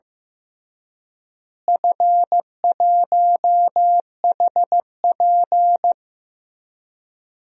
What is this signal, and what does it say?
Morse code, heard as a keyed tone, about 15 wpm, 700 Hz. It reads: F1HP
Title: DXCC-100 Call Signs - 4 Characters - Repeated 3x - 15wpm